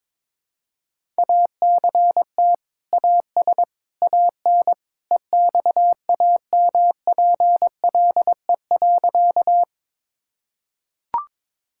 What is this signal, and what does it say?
Morse code, heard as a keyed tone, about 21 wpm, 700 Hz. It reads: ACT AS AN EXAMPLE.
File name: practice / 22wpm / Sentences with 2-Word Phrases - 22wpm